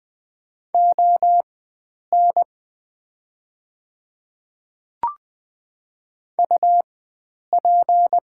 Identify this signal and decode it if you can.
Morse code, heard as a keyed tone, about 20 wpm, spaced as for 9 wpm, 700 Hz. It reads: ON UP
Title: 4x Character Spacing - Top 100 Words - 20wpm